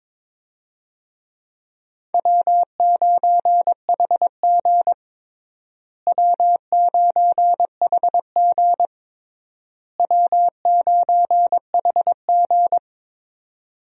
Morse code, heard as a keyed tone, about 21 wpm, 700 Hz. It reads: W9HG W9HG W9HG
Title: USA Call Signs - 4 Characters - Repeated 3x - 22wpm